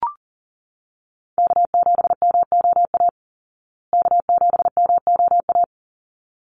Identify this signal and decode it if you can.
Morse code, heard as a keyed tone, about 40 wpm, 700 Hz. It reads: K7MOA K7MOA
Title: Call Signs Repeated 3x - 40wpm